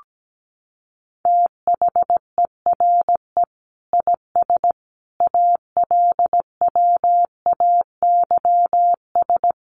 Morse code, heard as a keyed tone, about 17 wpm, 700 Hz. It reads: THERE IS ALWAYS
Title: Sentences from Top 1000 Words 17wpm